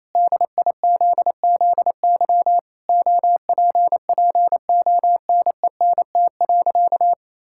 Morse code, heard as a keyed tone, about 28 wpm, 700 Hz. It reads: DIZZY OPPONENT.
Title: Pangram Challenge v2 - 28wpm